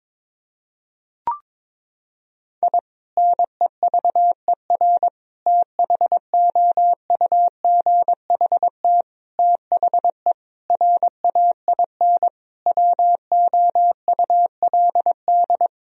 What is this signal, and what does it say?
Morse code, heard as a keyed tone, about 22 wpm, 700 Hz. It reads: I NEVER THOUGHT THE RAIN WOULD
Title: Sentences from Top 1000 Words - Review Part 1 - 22wpm